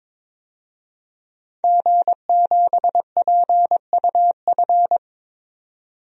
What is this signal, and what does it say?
Morse code, heard as a keyed tone, about 22 wpm, 700 Hz. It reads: G7PUF
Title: DXCC-100 Call Signs - 5 Characters - 22wpm